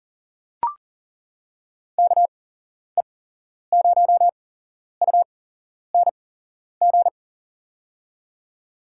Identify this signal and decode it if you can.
Morse code, heard as a keyed tone, about 40 wpm, spaced as for 11 wpm, 700 Hz. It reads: KE0UNG
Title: USA Call Signs - 6 Characters - 8x Character Spacing - 40wpm